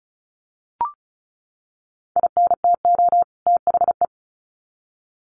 Morse code, heard as a keyed tone, about 34 wpm, 700 Hz. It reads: INTO THE
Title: Sets of 2 Words 35wpm